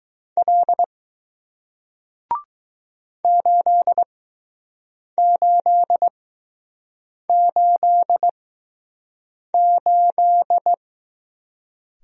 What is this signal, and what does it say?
Morse code, heard as a keyed tone, about 20 wpm, spaced as for 6 wpm, 700 Hz. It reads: L 8888